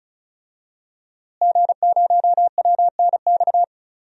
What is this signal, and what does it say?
Morse code, heard as a keyed tone, about 36 wpm, 700 Hz. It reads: G0WNX
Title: DXCC-100 Call Signs - 5 Characters - Repeated 3x - 35wpm